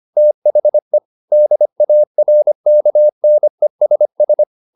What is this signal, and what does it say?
Morse code, heard as a keyed tone, about 25 wpm, 600 Hz. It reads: THE DARKNESS
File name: the old man and the sea part 1 25wpm